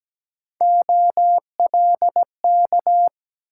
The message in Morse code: OLK

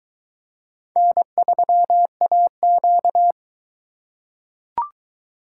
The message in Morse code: N3AQ

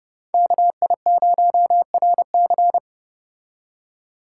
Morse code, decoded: KI0RC

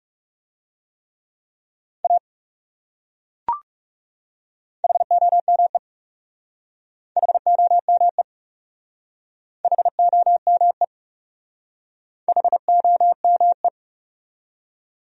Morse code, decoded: A HOME HOME HOME HOME

